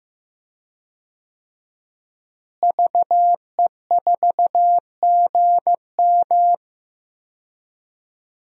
Morse code, heard as a keyed tone, about 15 wpm, 700 Hz. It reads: VE4GM
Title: DXCC-100 Call Signs - 5 Characters - Repeated 3x - 15wpm